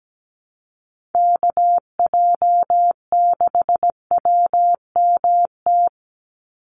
Morse code, heard as a keyed tone, about 17 wpm, 700 Hz. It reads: KJ6WMT
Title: USA Call Signs 17wpm